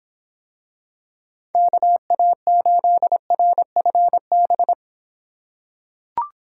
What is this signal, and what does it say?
Morse code, heard as a keyed tone, about 24 wpm, 700 Hz. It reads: KA8RFB